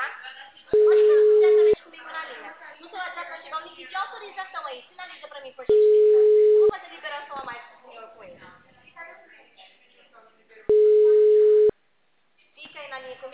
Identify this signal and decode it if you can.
Ringback tone (Europe, 425 Hz), with speech.
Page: Cordeiropolis